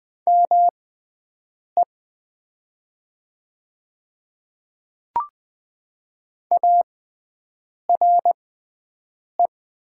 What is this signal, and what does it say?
Morse code, heard as a keyed tone, about 20 wpm, spaced as for 7 wpm, 700 Hz. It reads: ME ARE
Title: 6x Character Spacing - Top 500 Words - 20wpm